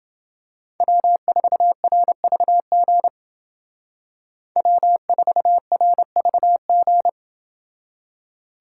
Morse code, heard as a keyed tone, about 29 wpm, 700 Hz. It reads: W4RVG W4RVG